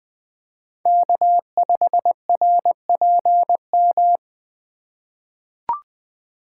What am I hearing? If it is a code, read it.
Morse code, heard as a keyed tone, about 20 wpm, 700 Hz. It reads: K5RPM
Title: USA Call Signs - 3x Character Spacing - 20wpm